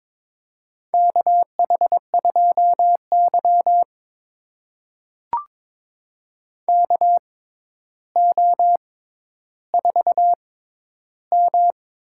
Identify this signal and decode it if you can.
Morse code, heard as a keyed tone, about 22 wpm, spaced as for 11 wpm, 700 Hz. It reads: KH2Y KO4M
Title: USA Call Signs - 4 Characters - 6x Character Spacing - 22wpm